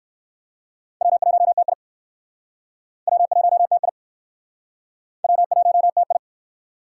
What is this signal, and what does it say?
Morse code, heard as a keyed tone, about 62 wpm, 700 Hz. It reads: W1TI W1TI W1TI